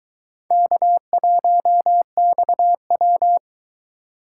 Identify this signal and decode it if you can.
Morse code, heard as a keyed tone, about 22 wpm, 700 Hz. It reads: K1XW